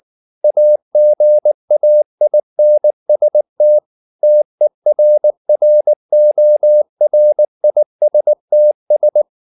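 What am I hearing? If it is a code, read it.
Morse code, heard as a keyed tone, about 19 wpm, 600 Hz. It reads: AGAINST TERRORISTS
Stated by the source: Morse code at 19 wpm from CW med Gnister, reading Ken Follett's novel Never in English.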